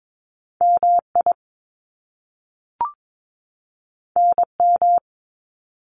Morse code, heard as a keyed tone, about 22 wpm, 700 Hz. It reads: MI NM
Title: US States 22wpm